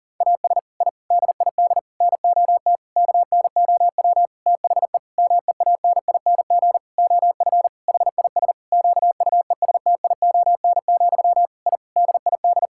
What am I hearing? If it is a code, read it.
Morse code, heard as a keyed tone, about 40 wpm, 700 Hz. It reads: AS I DID NOT KNOW THE MEANING OF HIS QUESTION, I DID
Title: Sentences from Top 1000 Words - Review Part 6 - 40wpm